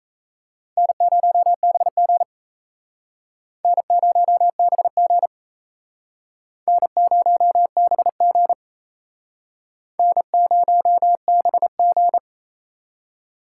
Morse code, heard as a keyed tone, about 38 wpm, 700 Hz. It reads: N0BG N0BG N0BG N0BG